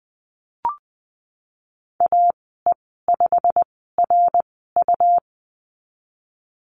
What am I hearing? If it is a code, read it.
Morse code, heard as a keyed tone, about 20 wpm, spaced as for 14 wpm, 700 Hz. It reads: AE5RU
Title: 147 Lesson 24g - Letter y - Call Signs 20wpm